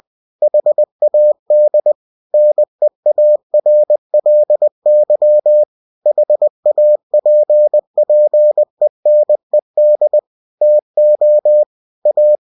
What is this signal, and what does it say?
Morse code, heard as a keyed tone, about 20 wpm, 600 Hz. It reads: HAD NEARLY HAPPENED TO A
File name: cw-books / winnie-the-pooh / winnie-20wpm-page-0106